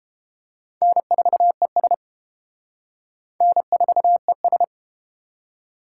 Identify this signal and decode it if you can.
Morse code, heard as a keyed tone, about 30 wpm, 700 Hz. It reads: N4ES N4ES